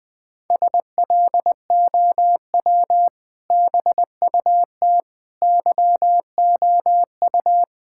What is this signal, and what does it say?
Morse code, heard as a keyed tone, about 21 wpm, 700 Hz. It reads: SLOW BUT YOU
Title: Sentences with 2-Word Phrases - 20wpm